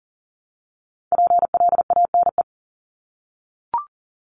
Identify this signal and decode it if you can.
Morse code, heard as a keyed tone, about 40 wpm, 700 Hz. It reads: PLANE
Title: Top 500 Words 40wpm